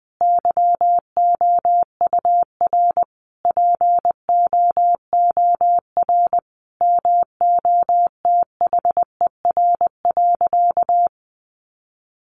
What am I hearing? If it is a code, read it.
Morse code, heard as a keyed tone, about 20 wpm, 700 Hz. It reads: YOUR POOR MOTHER.
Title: Sentences from Top 1000 Words 20wpm